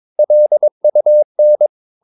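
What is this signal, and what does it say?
Morse code, heard as a keyed tone, about 22 wpm, 600 Hz. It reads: LUN
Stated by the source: Morse practice of short Danish words at 22 wpm, from CW med Gnister.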